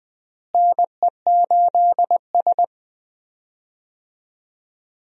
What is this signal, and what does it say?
Morse code, heard as a keyed tone, about 20 wpm, 700 Hz. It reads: NE8S
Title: USA Call Signs - 4 Characters - Repeated 3x - 20wpm